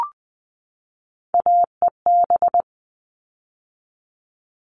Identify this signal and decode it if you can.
Morse code, heard as a keyed tone, about 20 wpm, 700 Hz. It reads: AEB
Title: All Cut Numbers - 3-Digit 20wpm